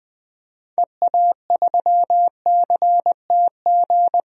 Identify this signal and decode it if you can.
Morse code, heard as a keyed tone, about 20 wpm, 700 Hz. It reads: EA3CTG
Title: DXCC-100 Call Signs - 6 Characters - Repeated 3x - 20wpm